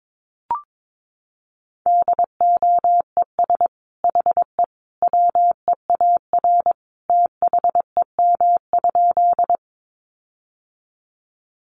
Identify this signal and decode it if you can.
Morse code, heard as a keyed tone, about 22 wpm, 700 Hz. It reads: DOES HE WEAR THEM?
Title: Sentences from Top 1500 Words 22wpm